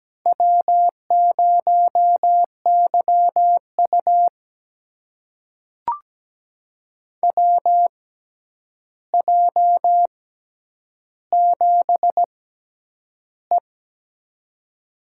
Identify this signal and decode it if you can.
Morse code, heard as a keyed tone, about 17 wpm, spaced as for 9 wpm, 700 Hz. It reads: W0YU WJ7E
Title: USA Call Signs - 4 Characters - 6x Character Spacing - 17wpm